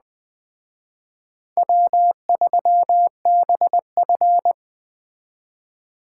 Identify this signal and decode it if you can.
Morse code, heard as a keyed tone, about 23 wpm, 700 Hz. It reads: W3BF
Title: Speed Racing - USA 1x2 Call Signs 20wpm